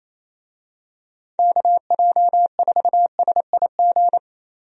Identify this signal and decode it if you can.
Morse code, heard as a keyed tone, about 28 wpm, 700 Hz. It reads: KJ4SIG
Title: USA Call Signs - 6 Characters - 28wpm